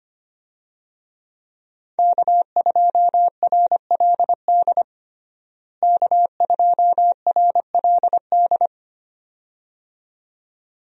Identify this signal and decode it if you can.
Morse code, heard as a keyed tone, about 25 wpm, 700 Hz. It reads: K2RLD K2RLD
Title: USA Call Signs - 5 Characters - Repeated 2x - 25wpm